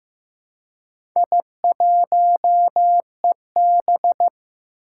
Morse code, heard as a keyed tone, about 15 wpm, 700 Hz. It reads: I1EB
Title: Four Challenging Letter-Number Combinations 15wpm